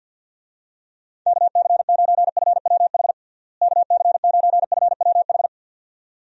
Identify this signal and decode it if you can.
Morse code, heard as a keyed tone, about 53 wpm, 700 Hz. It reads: KC9FWH KC9FWH
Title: USA Call Signs - 6 Characters - Repeated 2x - 50wpm